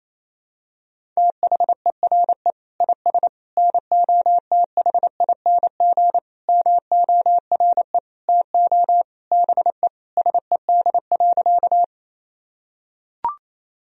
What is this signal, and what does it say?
Morse code, heard as a keyed tone, about 28 wpm, 700 Hz. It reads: THERE IS NOTHING MORE TO BE SED.
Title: Sentences from Top 500 Words and Abbreviations 28wpm